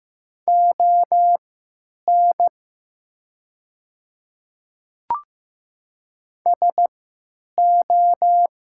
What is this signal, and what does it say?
Morse code, heard as a keyed tone, about 16 wpm, spaced as for 9 wpm, 700 Hz. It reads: ON SO
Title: 3x Character Spacing - Top 300 Words - 15wpm